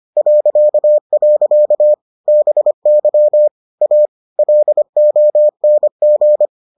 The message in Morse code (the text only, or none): .. BY A LONG